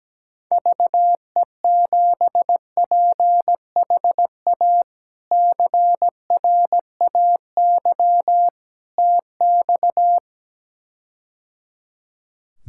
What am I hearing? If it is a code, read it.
Morse code, heard as a keyed tone, about 17 wpm, 700 Hz. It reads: VE7PHA CRAY TX
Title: K1USN SST QSO Practice - 17wpm